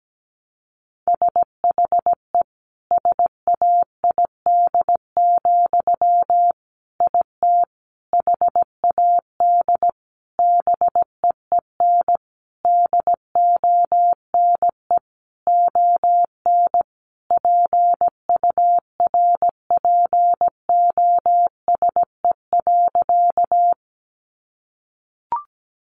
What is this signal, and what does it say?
Morse code, heard as a keyed tone, about 17 wpm, 700 Hz. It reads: SHE SAID, IT HAD BEEN DONE ON PURPOSE.
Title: Sentences from Top 800 Words 17wpm